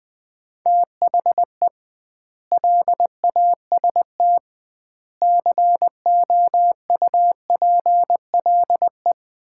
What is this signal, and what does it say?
Morse code, heard as a keyed tone, about 20 wpm, 700 Hz. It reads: THE LAST COUPLE
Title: 2x Word-Spacing - Sets of 5 Words - 20wpm